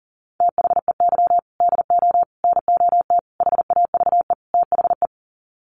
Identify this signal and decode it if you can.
Morse code, heard as a keyed tone, about 40 wpm, 700 Hz. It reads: THEY DO NOT HAVE THE
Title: Sets of 5 Words 40wpm